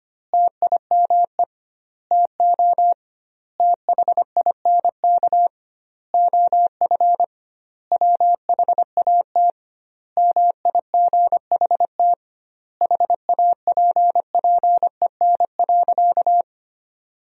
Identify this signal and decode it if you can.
Morse code, heard as a keyed tone, about 25 wpm, 700 Hz. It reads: TIME TO THINK OF WHAT MIGHT HAPPEN.
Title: 2x Word-Spacing - Sentences from Top 400 Words - 25wpm